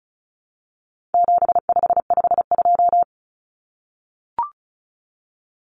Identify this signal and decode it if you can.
Morse code, heard as a keyed tone, about 34 wpm, 700 Hz. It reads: 7552
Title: Common Cut Numbers - 4-Digit 35wpm